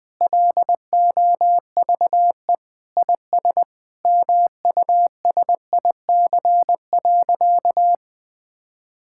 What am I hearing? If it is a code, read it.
Morse code, heard as a keyed tone, about 20 wpm, 700 Hz. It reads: LOVE IS MUSIC.
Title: Sentences with 5-Word Phrases - Spoken Beginning - 20wpm